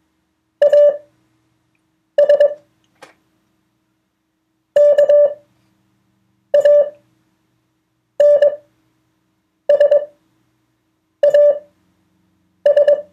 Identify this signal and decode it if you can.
Morse code, heard as a keyed tone, about 21 wpm, spaced as for 6 wpm, 580 Hz. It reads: AS KANSAS